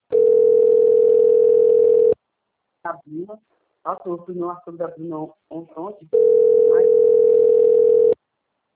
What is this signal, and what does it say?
Ringback tone (North America), 440 + 480 Hz, with speech.
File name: rg-1600-unknown- - - .16783